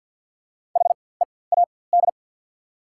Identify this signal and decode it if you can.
Morse code, heard as a keyed tone, about 53 wpm, spaced as for 24 wpm, 700 Hz. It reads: HEAD
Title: 4x Character Spacing - Top 300 Words - 50wpm